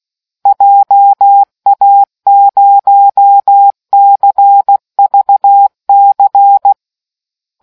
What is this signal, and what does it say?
Morse code, heard as a keyed tone, about 16 wpm, 800 Hz. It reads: JA0CVC